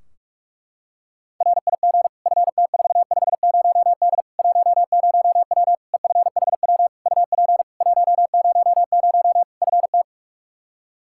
Morse code, heard as a keyed tone, about 41 wpm, 700 Hz. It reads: RIG FT450D 10W EFHW UP 100 FT
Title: K1USN SST QSO Practice 45wpm